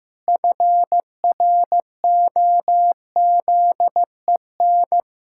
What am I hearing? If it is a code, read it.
Morse code, heard as a keyed tone, about 16 wpm, 700 Hz. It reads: FROZEN
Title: Pangram Challenge v2 - 15wpm